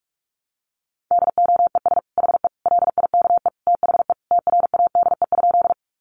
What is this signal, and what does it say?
Morse code, heard as a keyed tone, about 44 wpm, 700 Hz. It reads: DOES HE LIKE THE TRADE?
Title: Sentences from Top 600 Words 45wpm